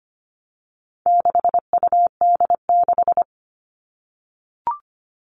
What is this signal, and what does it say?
Morse code, heard as a keyed tone, about 25 wpm, 700 Hz. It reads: 6UD6
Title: All Cut Numbers - 4-Digit 25wpm